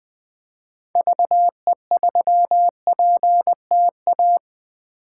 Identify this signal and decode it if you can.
Morse code, heard as a keyed tone, about 21 wpm, 700 Hz. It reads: VE3PTA
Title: 2023 POTA Hunter Call Signs - 20wpm